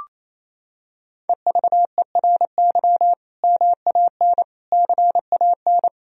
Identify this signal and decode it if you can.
Morse code, heard as a keyed tone, about 28 wpm, 700 Hz. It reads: EVERY MAN CAN
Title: Sentences from Top 1000 Words - Review Part 1 - 28wpm